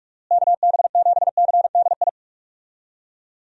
Morse code, heard as a keyed tone, about 45 wpm, 700 Hz. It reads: KB7CDI
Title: USA Call Signs - 6 Characters - Repeated 3x - 45wpm